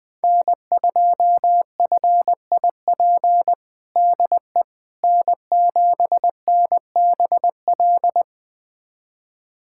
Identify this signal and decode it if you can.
Morse code, heard as a keyed tone, about 21 wpm, 700 Hz. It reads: N2FIP DE N7NBL
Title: USA Callsign DE Callsign - 5 Characters - 20wpm